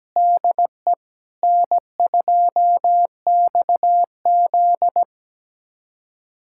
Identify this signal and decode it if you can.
Morse code, heard as a keyed tone, about 17 wpm, 700 Hz. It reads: DE N2XZ
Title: USA Callsign DE Callsign - 4 Characters - 17wpm